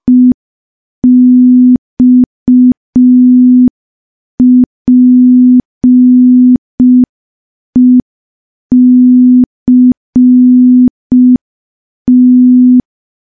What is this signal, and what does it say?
Morse code, heard as a keyed tone, about 5 wpm, 250 Hz. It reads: EXPECT
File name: morse / morse-training-05WPM